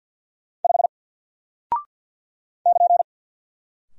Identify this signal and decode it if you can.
Morse code, heard as a keyed tone, about 50 wpm, 700 Hz. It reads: 5 <KN>
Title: Single Character (Letter:Number:Punctuation:Prosign) 50wpm